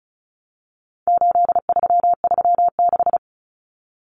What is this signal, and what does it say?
Morse code, heard as a keyed tone, about 34 wpm, 700 Hz. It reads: 8336
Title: Common Cut Numbers - 4-Digit 35wpm